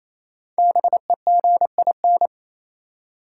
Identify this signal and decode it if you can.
Morse code, heard as a keyed tone, about 28 wpm, 700 Hz. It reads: BEGIN
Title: Five Letter Words - 28wpm